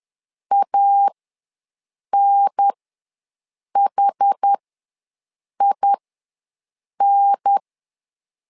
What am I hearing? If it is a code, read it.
Morse code, heard as a keyed tone, about 10 wpm, spaced as for 6 wpm, 800 Hz. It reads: ANHIN
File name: LEZIONI / 2 UHDN / 2 UHDN ws 10wpm 2044